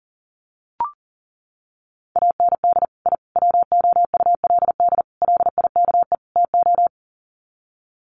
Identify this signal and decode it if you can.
Morse code, heard as a keyed tone, about 40 wpm, 700 Hz. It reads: AND I WOULD LIKE TO
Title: Sets of 5 Words Encore 40wpm